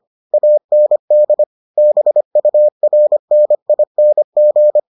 Morse code, heard as a keyed tone, about 25 wpm, 600 Hz. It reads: AND BURNING